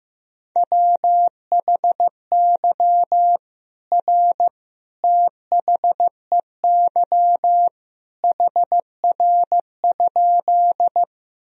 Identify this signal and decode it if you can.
Morse code, heard as a keyed tone, about 15 wpm, 700 Hz. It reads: WHY R THEY HR?
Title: Sentences from Top 200 Words and Abbreviations 15wpm